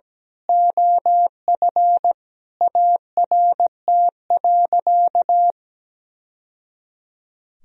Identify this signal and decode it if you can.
Morse code, heard as a keyed tone, about 17 wpm, 700 Hz. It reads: OF ART.
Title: Sentences from Top 1000 Words - Review Part 3 - 17wpm